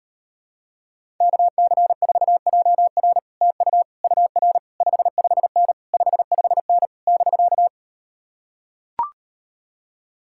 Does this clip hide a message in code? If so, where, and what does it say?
Yes: Morse code from 1.2 to 7.7 s, heard as a keyed tone, about 38 wpm, 700 Hz. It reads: KC4JR TU UR 55N 55N <BK>